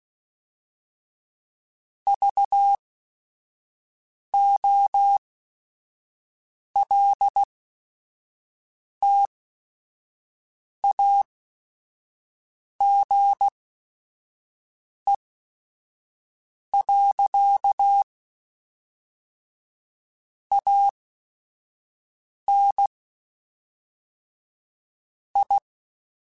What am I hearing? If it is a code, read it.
Morse code, heard as a keyed tone, about 16 wpm, spaced as for 5 wpm, 800 Hz. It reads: VOLTAGE. AN I
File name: Technical Subjects / Technical Subjects-mp3s / XC @5